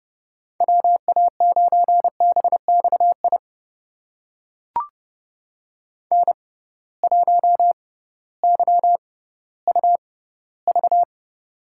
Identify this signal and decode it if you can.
Morse code, heard as a keyed tone, about 30 wpm, spaced as for 16 wpm, 700 Hz. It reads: WA9BXI N1YUV
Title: USA Call Signs - 6x Character Spacing - 30wpm